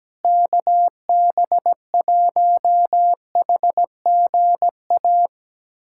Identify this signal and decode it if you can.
Morse code, heard as a keyed tone, about 17 wpm, 700 Hz. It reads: KB1HGA
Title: USA Callsign DE Callsign - 6 Characters - 17wpm